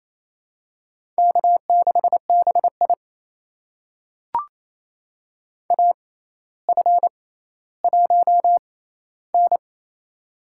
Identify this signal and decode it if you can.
Morse code, heard as a keyed tone, about 28 wpm, spaced as for 14 wpm, 700 Hz. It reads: K6BI AF1N
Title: USA Call Signs - 4 Characters - 6x Character Spacing - 28wpm